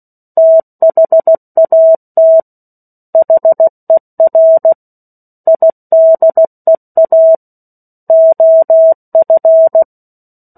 Morse code, heard as a keyed tone, about 16 wpm, 650 Hz. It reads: THAT HER IDEA OF
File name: 03-Chapter-03 16 WPM